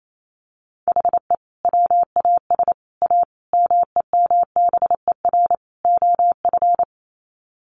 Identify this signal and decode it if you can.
Morse code, heard as a keyed tone, about 28 wpm, 700 Hz. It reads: HE WAS A MEMBER OF
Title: Sets of 5 Words Encore2 28wpm